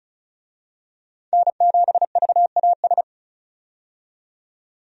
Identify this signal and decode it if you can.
Morse code, heard as a keyed tone, about 36 wpm, 700 Hz. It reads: N7VAS
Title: USA Call Signs - 5 Characters - 35wpm